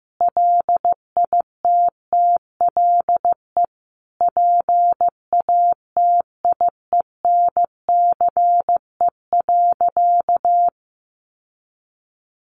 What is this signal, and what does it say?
Morse code, heard as a keyed tone, about 15 wpm, 700 Hz. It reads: LITTLE PATIENCE.
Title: Sentences from Top 1300 Words 15wpm